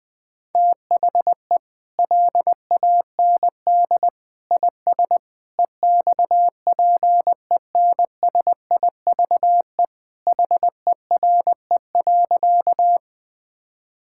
Morse code, heard as a keyed tone, about 20 wpm, 700 Hz. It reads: THE LAND IS EXPENSIVE HERE.
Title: Sentences from Top 1000 Words - Review Part 6 - 20wpm